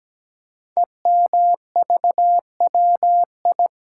Morse code, heard as a keyed tone, about 17 wpm, 700 Hz. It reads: EMVWI
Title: Code Groups - 5 Letters - 17wpm